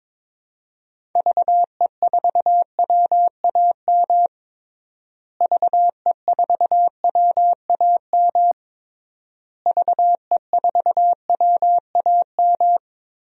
Morse code, heard as a keyed tone, about 22 wpm, 700 Hz. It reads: VE4WAM VE4WAM VE4WAM
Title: DXCC-100 Call Signs - 6 Characters - Repeated 3x - 22wpm